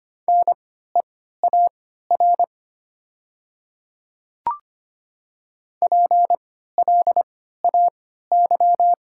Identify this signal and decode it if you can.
Morse code, heard as a keyed tone, about 25 wpm, spaced as for 14 wpm, 700 Hz. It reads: NEAR PLAY
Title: 3x Character Spacing - Top 200 Words - 25wpm